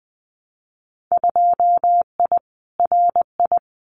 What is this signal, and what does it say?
Morse code, heard as a keyed tone, about 20 wpm, 700 Hz. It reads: 2I RI
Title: Winter Field Day Exchanges Encore 20wpm